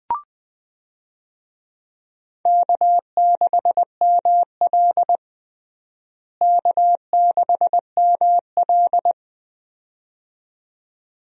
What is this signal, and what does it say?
Morse code, heard as a keyed tone, about 20 wpm, 700 Hz. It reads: K6ML K6ML
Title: USA Call Signs - 4 Characters - Repeated 2x - 20wpm